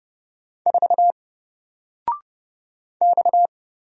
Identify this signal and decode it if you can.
Morse code, heard as a keyed tone, about 30 wpm, 700 Hz. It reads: 4 X
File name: Single-Letter-Number-30wpm